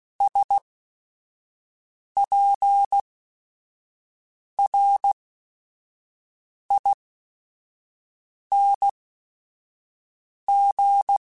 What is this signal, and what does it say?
Morse code, heard as a keyed tone, about 16 wpm, spaced as for 5 wpm, 800 Hz. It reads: SPRING